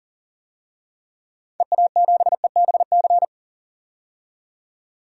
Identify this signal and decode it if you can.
Morse code, heard as a keyed tone, about 40 wpm, 700 Hz. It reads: EA7EBC
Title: DXCC-100 Call Signs - 6 Characters - 40wpm